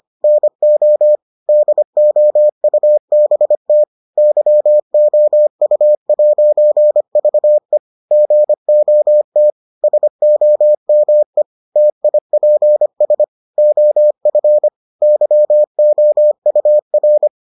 Morse code, heard as a keyed tone, about 25 wpm, 600 Hz. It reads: NO DOUBT YOU'VE GOT SOME TIPS OF YOUR